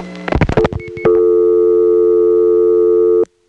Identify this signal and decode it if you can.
Dial tone (North America), 350 + 440 Hz.